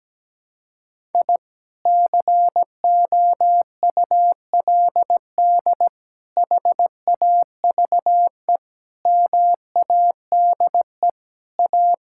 Morse code, heard as a keyed tone, about 17 wpm, 700 Hz. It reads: I COULD HAVE MADE A
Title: Sentences from Top 1000 Words - Review Part 5 - 17wpm